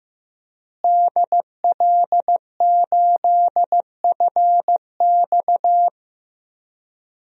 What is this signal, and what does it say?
Morse code, heard as a keyed tone, about 15 wpm, 700 Hz. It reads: DL8FX
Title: DXCC-100 Call Signs - 5 Characters - Repeated 3x - 15wpm